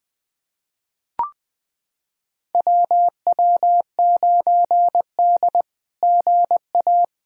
Morse code, heard as a keyed tone, about 20 wpm, 700 Hz. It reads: WW9D GA
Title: SOTA QSO Practice 20wpm